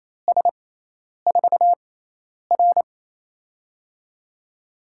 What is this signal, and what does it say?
Morse code, heard as a keyed tone, about 28 wpm, spaced as for 10 wpm, 700 Hz. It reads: S4R
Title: USA Call Signs - 4 Characters - 6x Character Spacing - 28wpm